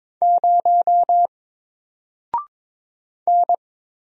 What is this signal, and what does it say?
Morse code, heard as a keyed tone, about 22 wpm, 700 Hz. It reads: 0 N
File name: Single-Letter-Number-22wpm